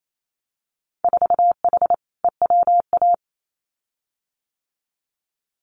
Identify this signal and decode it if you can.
Morse code, heard as a keyed tone, about 28 wpm, 700 Hz. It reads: 4H EWA
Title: Winter Field Day Exchanges Encore 28wpm